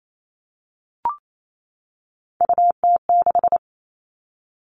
Morse code, heard as a keyed tone, about 28 wpm, 700 Hz. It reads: UT6
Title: All Cut Numbers - 3-Digit 28wpm